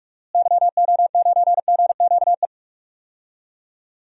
Morse code, heard as a keyed tone, about 46 wpm, 700 Hz. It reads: YO9GQE
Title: DXCC-100 Call Signs - 6 Characters - Repeated 2x - 45wpm